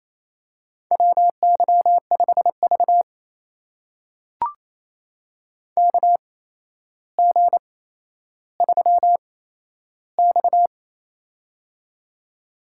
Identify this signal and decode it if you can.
Morse code, heard as a keyed tone, about 27 wpm, spaced as for 12 wpm, 700 Hz. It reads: WY5V KG3X
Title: USA Call Signs - 4 Characters - 8x Character Spacing - 28wpm